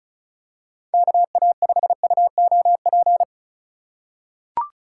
Morse code, heard as a keyed tone, about 36 wpm, 700 Hz. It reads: KA5UOP
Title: USA Call Signs - 6 Characters - 3x Character Spacing - 35wpm